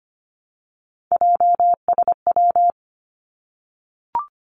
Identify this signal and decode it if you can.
Morse code, heard as a keyed tone, about 25 wpm, 700 Hz. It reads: JSW
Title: Three Challenging Letter-Number Combinations 25wpm